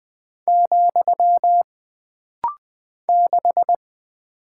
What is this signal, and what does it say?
Morse code, heard as a keyed tone, about 20 wpm, 700 Hz. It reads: , 6
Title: Single Character (Letter:Number:Punctuation:Prosign) - Rapid-Fire - 20wpm